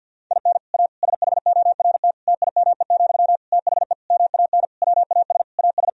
Morse code, heard as a keyed tone, about 46 wpm, 700 Hz. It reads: IN A SHORT TIME, THE MAN WAS AS
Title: Sentences from Top 1000 Words - Review Part 2 - 50wpm